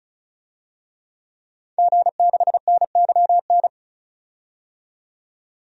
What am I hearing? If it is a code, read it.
Morse code, heard as a keyed tone, about 34 wpm, 700 Hz. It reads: G6NYN
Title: DXCC-100 Call Signs - 5 Characters - 35wpm